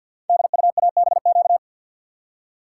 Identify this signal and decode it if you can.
Morse code, heard as a keyed tone, about 53 wpm, 700 Hz. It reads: DUABQ
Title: Code Groups - 5 Letters - 50wpm